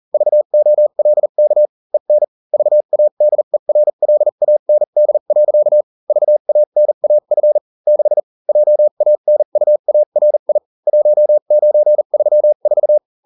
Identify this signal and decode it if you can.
Morse code, heard as a keyed tone, about 40 wpm, spaced as for 35 wpm, 600 Hz. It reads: VOLK EN VADERLAND. VANAF 6 JANUARI 1934